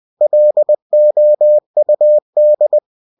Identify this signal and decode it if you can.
Morse code, heard as a keyed tone, about 20 wpm, 600 Hz. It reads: LOUD